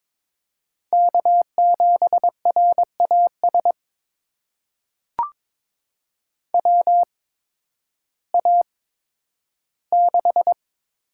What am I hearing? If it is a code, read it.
Morse code, heard as a keyed tone, about 22 wpm, spaced as for 12 wpm, 700 Hz. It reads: K7RAS WA6
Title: USA Call Signs - 8x Character Spacing - 22wpm